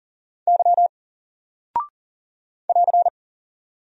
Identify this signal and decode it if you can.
Morse code, heard as a keyed tone, about 40 wpm, 700 Hz. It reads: Y <AR>
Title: Single Character (Letter:Number:Punctuation:Prosign) Alt 40wpm